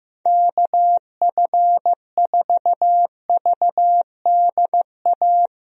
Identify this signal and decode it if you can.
Morse code, heard as a keyed tone, about 15 wpm, 700 Hz. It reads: KF4VDA
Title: USA Call Signs - 6 Characters - Repeated 2x - 15wpm